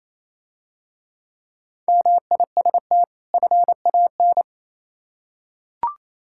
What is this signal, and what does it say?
Morse code, heard as a keyed tone, about 27 wpm, 690 Hz. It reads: MIST FAN
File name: Easy Words - Speed-Racing with Hints - 22wpm